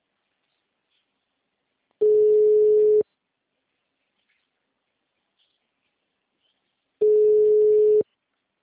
Ringback tone (Europe, 425 Hz).